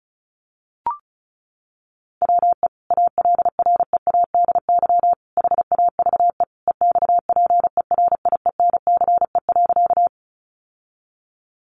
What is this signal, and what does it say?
Morse code, heard as a keyed tone, about 34 wpm, 700 Hz. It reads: WE ALREADY HAVE EXPERIENCE.
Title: Sentences from Top 700 Words 35wpm